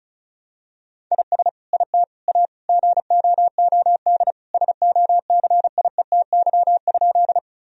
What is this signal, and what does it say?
Morse code, heard as a keyed tone, about 36 wpm, 700 Hz. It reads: IS IT A GOOD SOCIETY?
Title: Sentences from Top 1000 Words - Review Part 4 - 35wpm